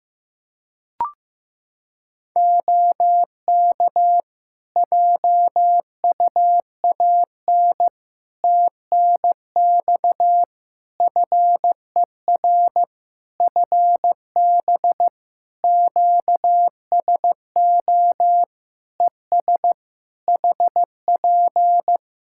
Morse code, heard as a keyed tone, about 16 wpm, 700 Hz. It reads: OK JUAN TNX FER FB QSO ES HP
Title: Practice QSO - Exchange 4 - Ending - 15wpm